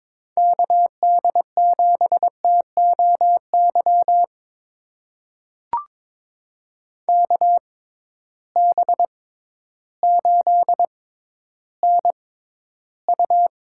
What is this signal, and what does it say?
Morse code, heard as a keyed tone, about 22 wpm, spaced as for 12 wpm, 700 Hz. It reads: KD7TOY KB8NU
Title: USA Call Signs - 6 Characters - 6x Character Spacing - 22wpm